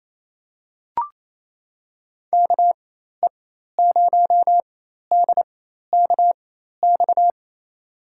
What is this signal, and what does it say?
Morse code, heard as a keyed tone, about 28 wpm, spaced as for 13 wpm, 700 Hz. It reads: KE0DKX
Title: USA Call Signs - 4x Character Spacing - 28wpm